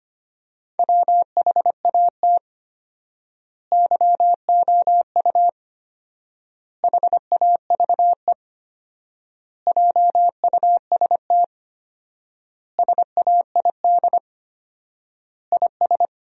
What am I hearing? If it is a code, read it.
Morse code, heard as a keyed tone, about 25 wpm, 700 Hz. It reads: WHAT YOU HAVE JUST SAID IS
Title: 4x Word-Spacing - Sentences from Top 200 Words - 25wpm